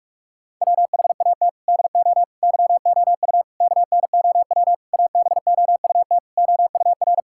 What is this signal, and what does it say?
Morse code, heard as a keyed tone, about 46 wpm, 700 Hz. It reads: WHAT DO YOU KNOW ABOUT OUR
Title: Sentences from Top 1000 Words - Review Part 6 - 45wpm